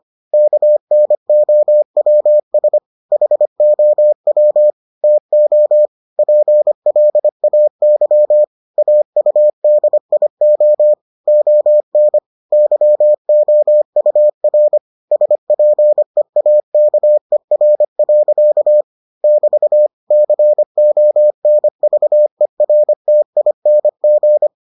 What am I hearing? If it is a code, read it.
Morse code, heard as a keyed tone, about 25 wpm, 600 Hz. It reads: KNOWS HOW TO PLAY AUDIO ON YOUR SPEAKER. <BT> CONVERTING